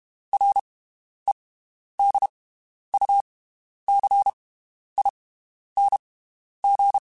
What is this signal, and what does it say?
Morse code, heard as a keyed tone, about 32 wpm, spaced as for 11 wpm, 800 Hz. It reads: REDUCING
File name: (spaced) / QSOs Spaced-mp3s / WD0FBT 1 @30 sp